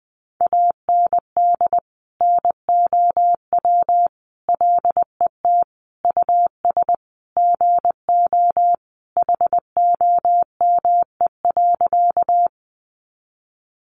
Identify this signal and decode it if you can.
Morse code, heard as a keyed tone, about 20 wpm, 700 Hz. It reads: AND NOW LET US GO HOME.
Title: Sentences from Top 300 Words 20wpm